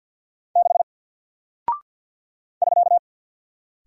Morse code, heard as a keyed tone, about 49 wpm, 700 Hz. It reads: 6 <SK>
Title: Single Character (Letter:Number:Punctuation:Prosign) Alt 50wpm